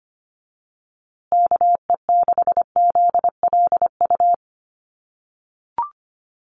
Morse code, heard as a keyed tone, about 25 wpm, 700 Hz. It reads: KE6ZLU
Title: USA Call Signs Repeated 2x 25wpm